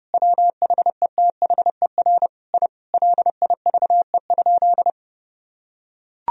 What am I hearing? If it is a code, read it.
Morse code, heard as a keyed tone, about 29 wpm, 700 Hz. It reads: WHETHER I LIVE?
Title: Sentences from Top 1000 Words - Review Part 4 - 30wpm